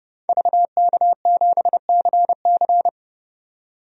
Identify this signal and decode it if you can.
Morse code, heard as a keyed tone, about 30 wpm, 700 Hz. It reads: VK7CC
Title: DXCC-100 Call Signs - 5 Characters - 30wpm